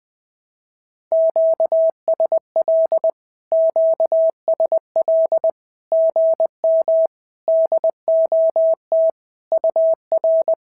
Morse code, heard as a keyed tone, about 20 wpm, 650 Hz. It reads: QSL QSL GM DOT UR
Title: SOTA QSO Practice 20wpm